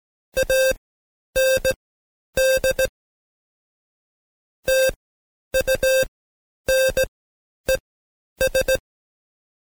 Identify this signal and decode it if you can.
Morse code, heard as a keyed tone, about 17 wpm, spaced as for 10 wpm, 530 Hz. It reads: AND TUNES
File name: Code Quick / AudioDownld2 / Seg131-Stepper4 9WPM